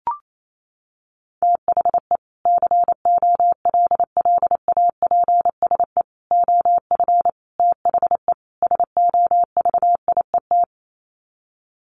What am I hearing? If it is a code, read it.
Morse code, heard as a keyed tone, about 28 wpm, 700 Hz. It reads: THE COLLAPSE OF THE SOVIET
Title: Sets of 5 Words Encore 28wpm